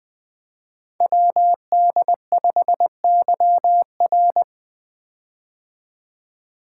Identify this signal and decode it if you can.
Morse code, heard as a keyed tone, about 20 wpm, 700 Hz. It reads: WD5YR
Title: Code Groups - 5 Letter:Numbers - 20wpm